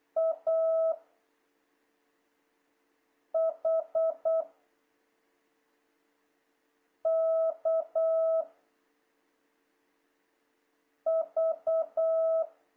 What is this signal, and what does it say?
Morse code, heard as a keyed tone, about 8 wpm, 660 Hz. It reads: A H K V
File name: jh8cbh / 2026cw